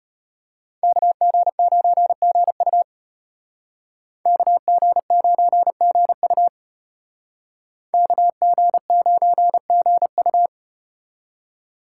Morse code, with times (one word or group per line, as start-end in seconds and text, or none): 0.8-2.8 KG9GU
4.3-6.5 KG9GU
7.9-10.5 KG9GU